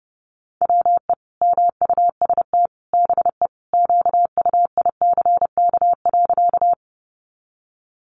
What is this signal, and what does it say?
Morse code, heard as a keyed tone, about 30 wpm, 700 Hz. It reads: WE MUST BE QUICK.
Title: Sentences from Top 500 Words 30wpm